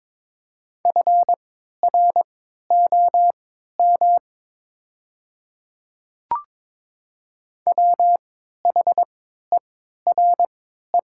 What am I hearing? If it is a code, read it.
Morse code, heard as a keyed tone, about 22 wpm, spaced as for 12 wpm, 700 Hz. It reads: FROM WHERE
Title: 3x Character Spacing - Top 500 Words - 22wpm